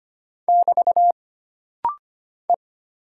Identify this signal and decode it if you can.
Morse code, heard as a keyed tone, about 25 wpm, 700 Hz. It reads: <BT> E
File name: Single Character (Letter:Number:Punctuation:Prosign) - Rapid-Fire - 25wpm